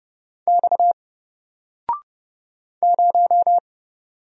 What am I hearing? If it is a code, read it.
Morse code, heard as a keyed tone, about 30 wpm, 700 Hz. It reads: X 0
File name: Single-Letter-Number-30wpm